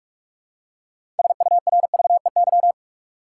Morse code, heard as a keyed tone, about 44 wpm, 700 Hz. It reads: SURVEY
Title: Six Letter Words - 45wpm